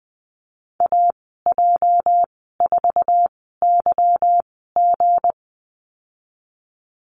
Morse code, heard as a keyed tone, about 20 wpm, spaced as for 15 wpm, 700 Hz. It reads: AJ4YG
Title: 219 Lesson 35g - Number 0 - Call Signs 20wpm